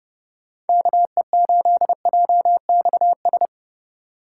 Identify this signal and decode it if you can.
Morse code, heard as a keyed tone, about 31 wpm, 700 Hz. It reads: KE8JXS
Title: USA Call Signs - 6 Characters - 2x Character Spacing - 30wpm